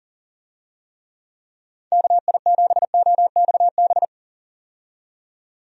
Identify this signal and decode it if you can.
Morse code, heard as a keyed tone, about 40 wpm, 700 Hz. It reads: KI7OXB
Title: USA Call Signs - 6 Characters - 40wpm